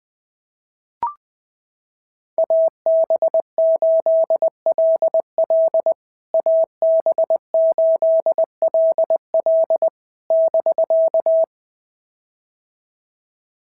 Morse code, heard as a keyed tone, about 20 wpm, 650 Hz. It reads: AB8LL AB8LL <BK>
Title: POTA QSO Practice - 20wpm